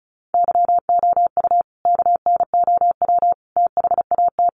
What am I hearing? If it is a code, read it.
Morse code, heard as a keyed tone, about 34 wpm, 700 Hz. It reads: YOU KNOW THAT
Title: Sentences from Top 1500 Words 35wpm